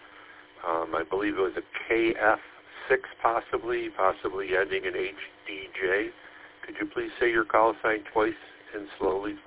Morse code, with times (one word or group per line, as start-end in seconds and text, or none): none